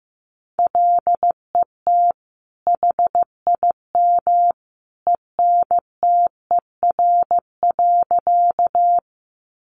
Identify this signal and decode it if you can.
Morse code, heard as a keyed tone, about 15 wpm, 700 Hz. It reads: LET HIM ENTER.
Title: Sentences from Top 1300 Words 15wpm